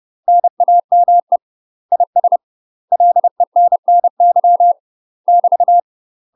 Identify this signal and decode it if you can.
Morse code, heard as a keyed tone, about 30 wpm, 700 Hz. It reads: NAME IS LENNY <BT>
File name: UK MASTER QSOs TRAINING / 30WPM / UK MASTER QSOs 0015